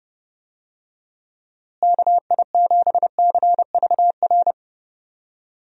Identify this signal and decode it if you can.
Morse code, heard as a keyed tone, about 32 wpm, 700 Hz. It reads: KI7CVR
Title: USA Call Signs - 6 Characters - Repeated 2x - 30wpm